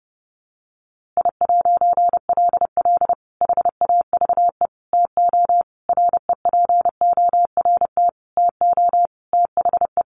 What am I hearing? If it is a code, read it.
Morse code, heard as a keyed tone, about 28 wpm, 700 Hz. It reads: I'LL HAVE TO REPORT TO THE
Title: A Sound Of Thunder 30wpm